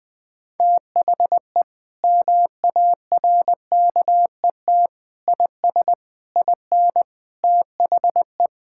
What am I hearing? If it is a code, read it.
Morse code, heard as a keyed tone, about 20 wpm, 700 Hz. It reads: THE MARKET IS IN THE
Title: Sentences from Top 1000 Words - Review Part 6 - 20wpm